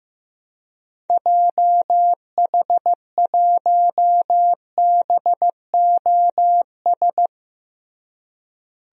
Morse code, heard as a keyed tone, about 14 wpm, 700 Hz. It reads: JH1BOS
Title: DXCC-100 Call Signs - 6 Characters - Repeated 2x - 15wpm